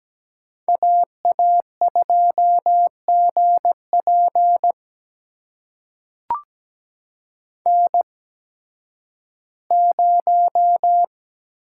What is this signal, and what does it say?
Morse code, heard as a keyed tone, about 17 wpm, spaced as for 11 wpm, 700 Hz. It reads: AA2GP N0
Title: USA Call Signs - 8x Character Spacing - 17wpm